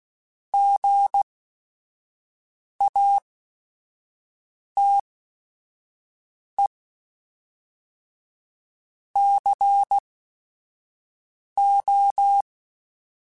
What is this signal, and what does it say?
Morse code, heard as a keyed tone, about 16 wpm, spaced as for 5 wpm, 800 Hz. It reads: GATE CO
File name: Technical Subjects